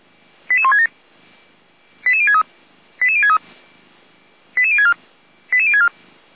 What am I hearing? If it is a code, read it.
Five-tone selective call (ZVEI), heard as quick runs of five tones, tones of about 70 ms. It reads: 88167 88063 88063 88064 88064
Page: SDR/ZVEI